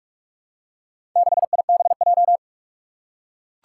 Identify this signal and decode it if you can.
Morse code, heard as a keyed tone, about 44 wpm, 700 Hz. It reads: 6IBJ
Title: Four Challenging Letter-Number Combinations 45wpm